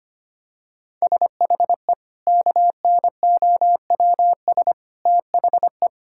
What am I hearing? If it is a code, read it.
Morse code, heard as a keyed tone, about 25 wpm, 700 Hz. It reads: SHE KNOWS THE
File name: practice / 25wpm / Sentences with 4-Word Phrases - 25wpm